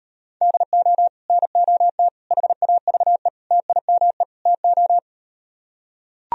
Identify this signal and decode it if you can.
Morse code, heard as a keyed tone, about 38 wpm, 700 Hz. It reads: DO NOT HAVE TIME TO